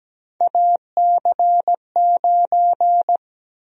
Morse code, heard as a keyed tone, about 17 wpm, 700 Hz. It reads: AC9
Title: USA Call Sign Prefixes - 17wpm